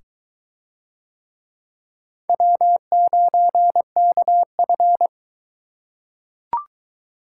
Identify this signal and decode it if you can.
Morse code, heard as a keyed tone, about 21 wpm, 700 Hz. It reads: W9KF